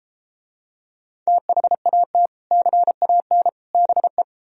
Morse code, heard as a keyed tone, about 33 wpm, 700 Hz. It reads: THAT CAN BE